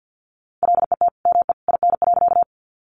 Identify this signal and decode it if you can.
Morse code, heard as a keyed tone, about 49 wpm, 700 Hz. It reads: LET ME IN.
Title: Sentences from Top 300 Words 50wpm